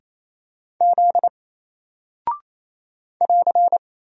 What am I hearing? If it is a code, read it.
Morse code, heard as a keyed tone, about 27 wpm, 700 Hz. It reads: Z <AR>
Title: Single Character (Letter:Number:Punctuation:Prosign) 28wpm